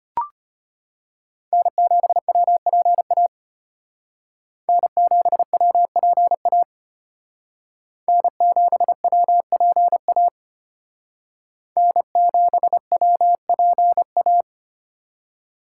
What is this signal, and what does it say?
Morse code, heard as a keyed tone, about 33 wpm, 700 Hz. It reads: N7WPA N7WPA N7WPA N7WPA